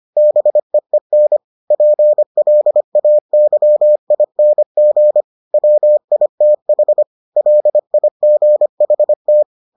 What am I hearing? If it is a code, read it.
Morse code, heard as a keyed tone, about 25 wpm, 600 Hz. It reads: BEEN PLAYING WITH LIGHT